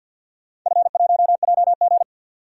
Morse code, heard as a keyed tone, about 46 wpm, 700 Hz. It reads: F1JG
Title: DXCC-100 Call Signs - 4 Characters - 50wpm